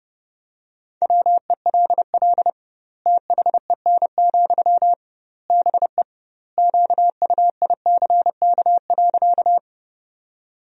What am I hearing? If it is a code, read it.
Morse code, heard as a keyed tone, about 28 wpm, 700 Hz. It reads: WELL THEN, BE QUICK.
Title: 2x Word-Spacing - Sentences from Top 500 Words - 30wpm